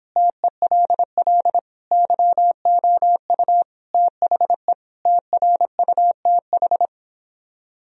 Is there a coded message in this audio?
Yes: Morse code from 0.2 to 6.9 s, heard as a keyed tone, about 26 wpm, 700 Hz. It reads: TELL YOU THE TRUTH